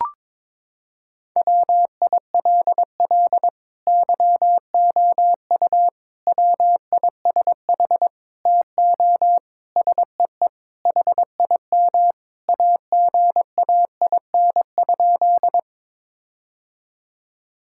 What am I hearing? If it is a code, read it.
Morse code, heard as a keyed tone, about 22 wpm, 700 Hz. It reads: WILL YOU WISH TO SEE HIM AGAIN?
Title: Sentences from Top 1600 Words 22wpm